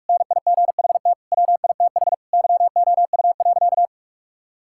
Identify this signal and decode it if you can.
Morse code, heard as a keyed tone, about 41 wpm, 700 Hz. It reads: NIGHT WITH YOU.
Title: Sentences from Top 1000 Words - Review Part 5 - 45wpm